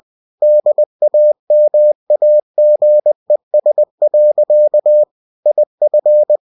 Morse code, heard as a keyed tone, about 20 wpm, 600 Hz. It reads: DAMAGES. IF
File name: cw-books / winnie-the-pooh / winnie-20wpm-page-0119